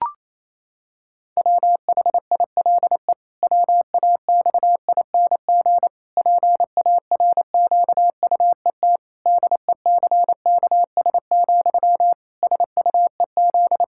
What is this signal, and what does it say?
Morse code, heard as a keyed tone, about 28 wpm, 700 Hz. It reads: WHILE WAXING PARQUET DECKS, SUEZ
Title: Pangram Challenge v2 - 28wpm